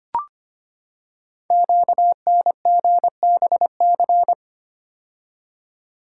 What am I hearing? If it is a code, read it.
Morse code, heard as a keyed tone, about 25 wpm, 700 Hz. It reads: QNGBC
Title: Code Groups - 5 Letters - 25wpm